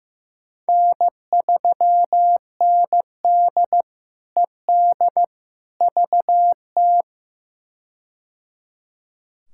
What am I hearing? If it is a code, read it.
Morse code, heard as a keyed tone, about 15 wpm, 700 Hz. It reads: N3ND ED VT
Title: K1USN SST QSO Practice - 15wpm